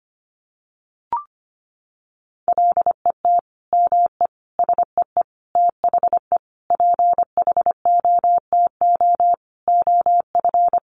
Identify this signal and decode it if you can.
Morse code, heard as a keyed tone, about 25 wpm, 700 Hz. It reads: LET ME SEE THE PHOTO OF
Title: Sentences from Top 1200 Words 25wpm